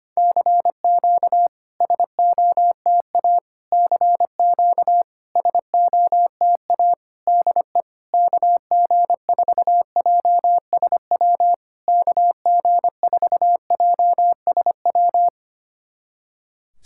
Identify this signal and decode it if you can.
Morse code, heard as a keyed tone, about 25 wpm, 700 Hz. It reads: CQ SOTA CQ SOTA DE KG4JSW KG4JSW
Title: SOTA QSO Practice 25wpm